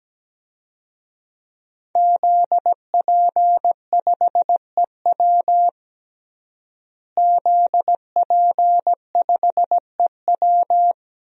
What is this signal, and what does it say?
Morse code, heard as a keyed tone, about 17 wpm, 700 Hz. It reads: ZP5EW ZP5EW
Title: DXCC-100 Call Signs - 5 Characters - Repeated 2x - 17wpm